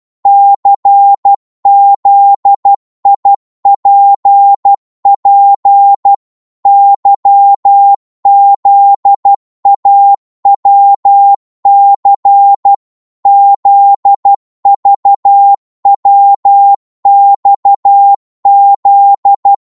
Morse code, heard as a keyed tone, about 12 wpm, 800 Hz. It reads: CZIPP YZAWC ZVWXZ
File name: Morse / ltr-12wpm